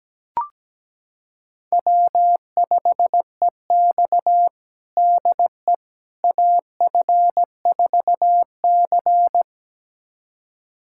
Morse code, heard as a keyed tone, about 17 wpm, 700 Hz. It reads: W5EX DE AF4C
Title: USA Callsign DE Callsign - 4 Characters - 17wpm